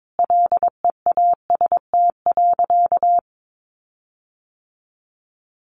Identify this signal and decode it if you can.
Morse code, heard as a keyed tone, about 22 wpm, 700 Hz. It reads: LEAST.
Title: Sentences from Top 1300 Words 22wpm